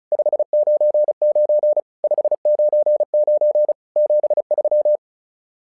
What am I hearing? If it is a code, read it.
Morse code, heard as a keyed tone, about 36 wpm, 600 Hz. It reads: 599 599 73
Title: SOTA QSO Practice 35wpm